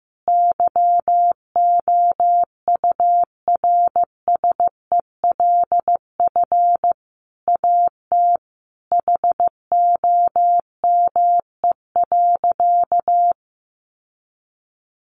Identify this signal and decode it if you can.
Morse code, heard as a keyed tone, about 15 wpm, 700 Hz. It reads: YOURSELF AT HOME.
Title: Sentences from Top 600 Words 15wpm